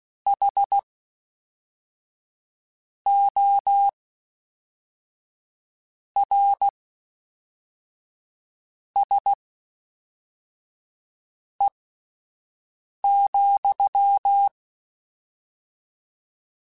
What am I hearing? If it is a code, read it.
Morse code, heard as a keyed tone, about 16 wpm, spaced as for 4 wpm, 800 Hz. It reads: HORSE,
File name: Misc / Misc-mp3s / WordPairs sp @5